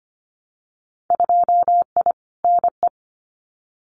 Morse code, heard as a keyed tone, about 25 wpm, 700 Hz. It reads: 2I NE
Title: Winter Field Day Exchanges Encore 25wpm